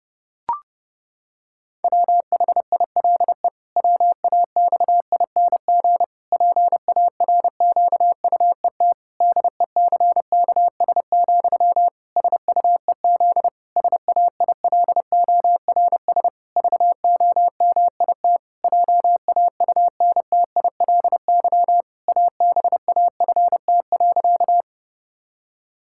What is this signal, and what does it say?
Morse code, heard as a keyed tone, about 30 wpm, 700 Hz. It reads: WHILE WAXING PARQUET DECKS, SUEZ SAILORS VOMIT JAUNTILY ABAFT.
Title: Pangram Challenge v2 - 30wpm